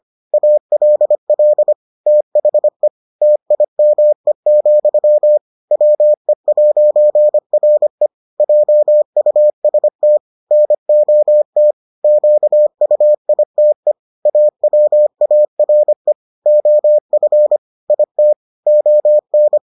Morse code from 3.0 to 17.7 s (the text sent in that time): TIME, WE'RE JUST NOT QUITE AWARE OF